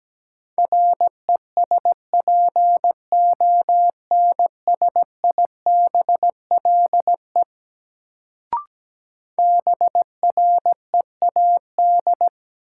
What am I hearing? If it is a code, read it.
Morse code, heard as a keyed tone, about 17 wpm, 700 Hz. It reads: RESPONSIBLE BREAD
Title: Sentences from Top 1600 Words - New Word Drill Alt - 17wpm